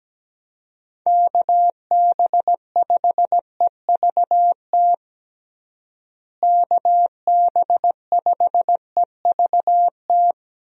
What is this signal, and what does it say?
Morse code, heard as a keyed tone, about 17 wpm, 700 Hz. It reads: KB5EVT KB5EVT
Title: USA Call Signs - 6 Characters - Repeated 3x - 17wpm